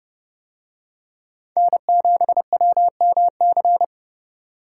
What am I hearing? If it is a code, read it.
Morse code, heard as a keyed tone, about 28 wpm, 700 Hz. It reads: N7WMC